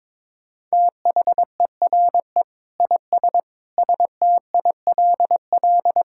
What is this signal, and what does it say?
Morse code, heard as a keyed tone, about 22 wpm, 700 Hz. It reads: THERE IS STILL
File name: Sentences with 9-Word Phrases - 22wpm